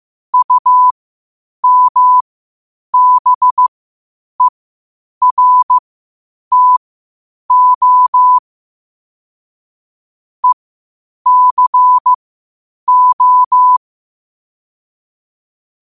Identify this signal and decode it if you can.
Morse code, heard as a keyed tone, about 15 wpm, spaced as for 8 wpm, 1000 Hz. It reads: UMBERTO ECO
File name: qotdfiles / 8wpm